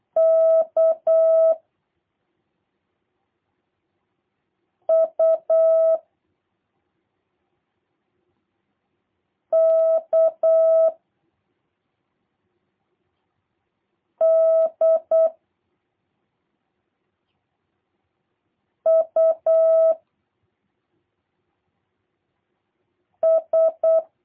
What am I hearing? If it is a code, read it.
Morse code, heard as a keyed tone, about 8 wpm, 650 Hz. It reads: K U K D U S
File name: jh8cbh / 2023cw / review1-3